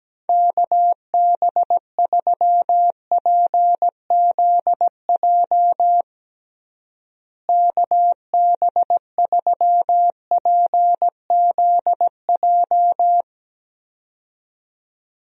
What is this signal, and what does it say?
Morse code, heard as a keyed tone, about 17 wpm, 700 Hz. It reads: KB3PZJ KB3PZJ
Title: USA Call Signs - 6 Characters - Repeated 2x - 17wpm